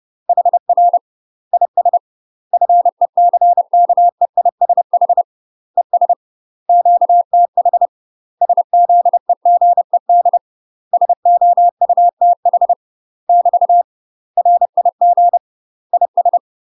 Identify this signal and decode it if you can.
Morse code, heard as a keyed tone, about 30 wpm, 700 Hz. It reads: HR IS FECKEISH ES QTH SZEGED SOUTH <BT> RIG IS
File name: UK MASTER QSOs TRAINING / 30WPM / UK MASTER QSOs 0038